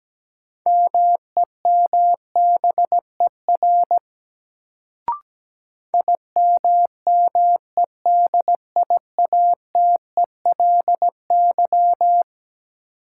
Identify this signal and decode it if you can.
Morse code, heard as a keyed tone, about 17 wpm, 700 Hz. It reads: MEMBER IMMEDIATELY
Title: Sentences from Top 1100 Words - New Word Drill Alt - 17wpm